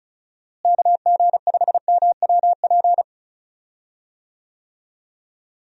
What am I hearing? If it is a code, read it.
Morse code, heard as a keyed tone, about 36 wpm, 700 Hz. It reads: KG5MWP
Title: USA Call Signs - 6 Characters - Repeated 3x - 35wpm